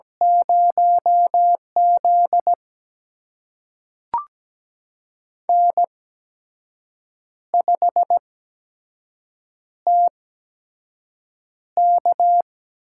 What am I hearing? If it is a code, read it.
Morse code, heard as a keyed tone, about 17 wpm, spaced as for 6 wpm, 700 Hz. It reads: 0Z N5TK
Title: USA Call Signs - 4 Characters - 8x Character Spacing - 17wpm